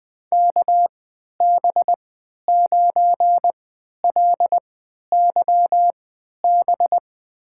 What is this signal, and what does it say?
Morse code, heard as a keyed tone, about 21 wpm, spaced as for 12 wpm, 700 Hz. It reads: KB9LYB
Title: USA Call Signs - 6 Characters - 3x Character Spacing - 20wpm